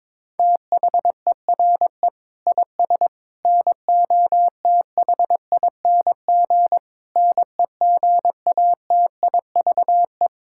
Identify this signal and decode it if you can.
Morse code, heard as a keyed tone, about 21 wpm, 700 Hz. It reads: THERE IS NOTHING NEGATIVE
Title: Sentences from Top 1000 Words - Review Part 6 - 22wpm